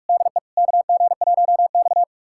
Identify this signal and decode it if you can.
Morse code, heard as a keyed tone, about 46 wpm, 700 Hz. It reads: DE KG1X
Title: USA Callsign DE Callsign - 4 Characters - 45wpm